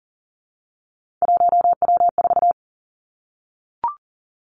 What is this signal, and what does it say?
Morse code, heard as a keyed tone, about 39 wpm, 700 Hz. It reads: 1W4
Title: Three Challenging Letter-Number Combinations 40wpm